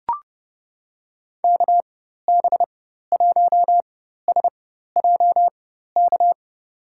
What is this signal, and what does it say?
Morse code, heard as a keyed tone, about 31 wpm, spaced as for 14 wpm, 700 Hz. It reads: KB1SJK
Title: USA Call Signs - 6 Characters - 4x Character Spacing - 30wpm